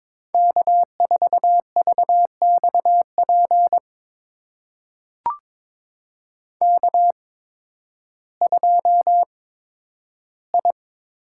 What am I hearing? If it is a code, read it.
Morse code, heard as a keyed tone, about 22 wpm, spaced as for 12 wpm, 700 Hz. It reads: K4VXP K2I
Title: USA Call Signs - 5 Characters - 8x Character Spacing - 22wpm